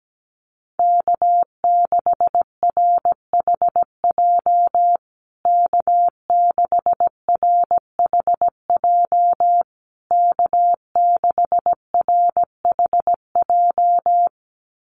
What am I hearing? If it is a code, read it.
Morse code, heard as a keyed tone, about 17 wpm, 700 Hz. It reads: K6RHJ K6RHJ K6RHJ
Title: USA Call Signs Repeated 3x 17wpm